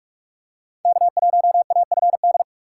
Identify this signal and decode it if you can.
Morse code, heard as a keyed tone, about 44 wpm, 700 Hz. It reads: K1AFD
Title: USA Call Signs - 6x Character Spacing - 45wpm